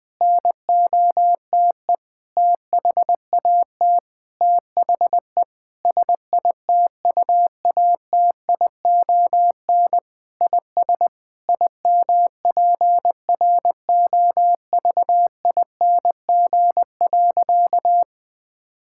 Morse code, heard as a keyed tone, about 20 wpm, 700 Hz. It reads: NOTE THAT THE SITUATION IS IMPROVING.
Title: Sentences with 8-Word Phrases - Spoken Beginning - 20wpm